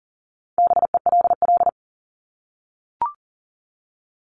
Morse code, heard as a keyed tone, about 40 wpm, 700 Hz. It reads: BELL
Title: Top 1751-2000 Words 40wpm